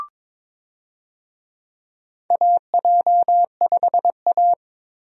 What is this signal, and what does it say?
Morse code, heard as a keyed tone, about 22 wpm, 700 Hz. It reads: AJ5A
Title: USA Call Signs - 4 Characters - Repeated 2x - 22wpm